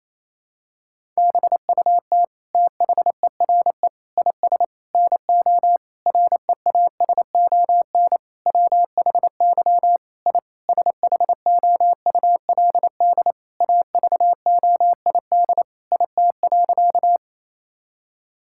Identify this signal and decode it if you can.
Morse code, heard as a keyed tone, about 28 wpm, 700 Hz. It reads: BUT THERE IS NO REASON WHY I SHOULD AVOID IT.
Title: Sentences from Top 1000 Words - Review Part 6 - 28wpm